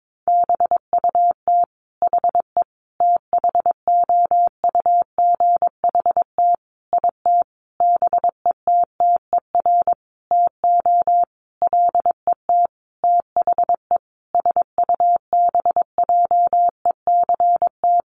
Morse code, heard as a keyed tone, about 22 wpm, 700 Hz. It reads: BUT HE THOUGHT IT BETTER TO LET THE SUBJECT
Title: Sentences from Top 1400 Words 22wpm